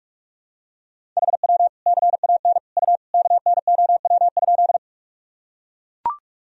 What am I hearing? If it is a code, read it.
Morse code, heard as a keyed tone, about 42 wpm, 700 Hz. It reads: HW CAN U KNOW?
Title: Sentences from Top 100 Words and Abbreviations 45wpm